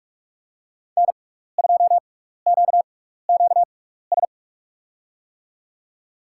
Morse code, heard as a keyed tone, about 47 wpm, spaced as for 16 wpm, 700 Hz. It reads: N2QQS
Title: USA Call Signs - 5 Characters - 6x Character Spacing - 45wpm